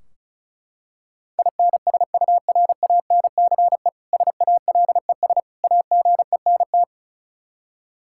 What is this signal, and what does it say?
Morse code, heard as a keyed tone, about 36 wpm, 700 Hz. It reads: INSURANCE SALES AGENT
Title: Practice QSO - Exchange 3 - More Info - 35wpm